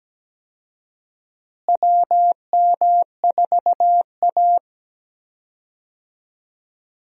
Morse code, heard as a keyed tone, about 17 wpm, 700 Hz. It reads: WM4A
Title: USA Call Signs - 4 Characters - Repeated 3x - 17wpm